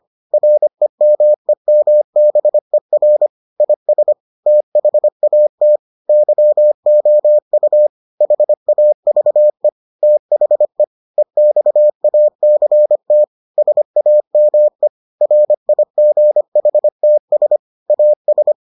Morse code, heard as a keyed tone, about 25 wpm, 600 Hz. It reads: REMEMBER IS THAT YOU HAVE THE EXACT SAME RIGHTS AS